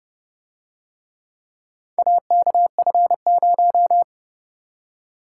Morse code, heard as a keyed tone, about 31 wpm, 700 Hz. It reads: AKF0